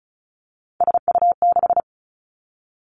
Morse code, heard as a keyed tone, about 36 wpm, 700 Hz. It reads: SU6
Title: Three Challenging Letter-Number Combinations 35wpm